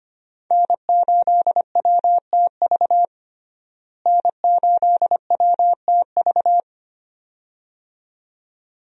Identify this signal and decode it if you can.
Morse code, heard as a keyed tone, about 25 wpm, 700 Hz. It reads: N8WTV N8WTV
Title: USA Call Signs - 5 Characters - Repeated 2x - 25wpm